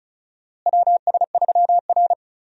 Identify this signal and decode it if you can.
Morse code, heard as a keyed tone, about 36 wpm, 700 Hz. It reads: WS3R
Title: USA Call Signs - 4 Characters - Repeated 3x - 35wpm